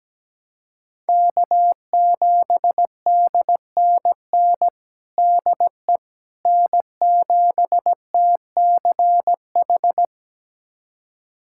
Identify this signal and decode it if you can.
Morse code, heard as a keyed tone, about 17 wpm, 700 Hz. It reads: K7DNN DE N7TCH
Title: USA Callsign DE Callsign - 5 Characters - 17wpm